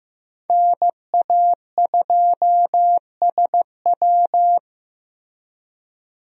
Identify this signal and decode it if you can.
Morse code, heard as a keyed tone, about 15 wpm, 700 Hz. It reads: NA2SW
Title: USA Call Signs - 5 Characters - Repeated 3x - 15wpm